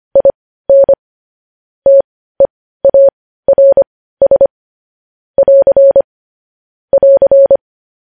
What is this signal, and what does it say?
Morse code, heard as a keyed tone, about 25 wpm, spaced as for 15 wpm, 550 Hz. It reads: IN TEARS <AR> <AR>
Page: news-headlines-in-morse-code-at-15-wpm